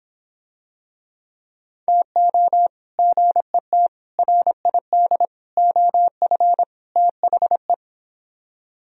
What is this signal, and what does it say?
Morse code, heard as a keyed tone, about 26 wpm, 700 Hz. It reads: TO GET RID OF THE